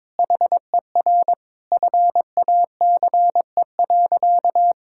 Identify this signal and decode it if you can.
Morse code, heard as a keyed tone, about 22 wpm, 700 Hz. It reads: HER FACE.
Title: Sentences from Top 1600 Words 22wpm